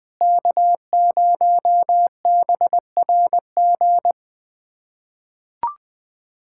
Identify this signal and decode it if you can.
Morse code, heard as a keyed tone, about 20 wpm, 700 Hz. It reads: K0BRG
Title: USA Callsign DE Callsign - 5 Characters - 20wpm